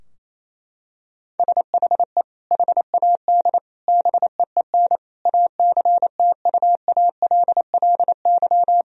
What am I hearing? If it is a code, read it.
Morse code, heard as a keyed tone, about 28 wpm, 700 Hz. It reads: SHE HAD BEEN ACTUALLY
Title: Sentences from Top 1000 Words - Review Part 4 - 28wpm